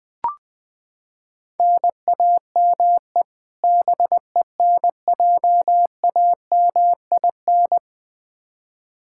Morse code, heard as a keyed tone, about 20 wpm, 700 Hz. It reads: NAME BENJAMIN
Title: K1USN SST QSO Practice 20wpm